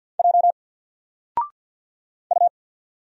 Morse code, heard as a keyed tone, about 50 wpm, 700 Hz. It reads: J U
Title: Single Letter-Number 50wpm